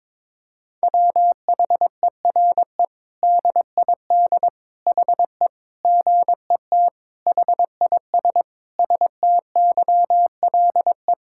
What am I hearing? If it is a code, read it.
Morse code, heard as a keyed tone, about 21 wpm, 700 Hz. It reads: WHERE DID HE GET HIS STYLE
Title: Sentences from Top 1000 Words - Review Part 5 - 22wpm